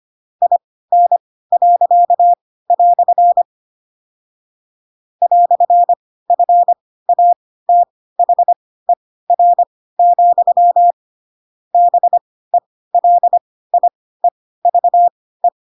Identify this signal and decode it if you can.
Morse code, heard as a keyed tone, about 25 wpm, spaced as for 16 wpm, 700 Hz. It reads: IN." "FATHER, BELIEVE
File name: Pinocchio / Pinocchio ch7 10 25 ik1hge